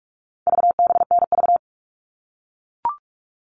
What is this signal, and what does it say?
Morse code, heard as a keyed tone, about 44 wpm, 700 Hz. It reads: VBNV
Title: All Cut Numbers - 4-Digit 45wpm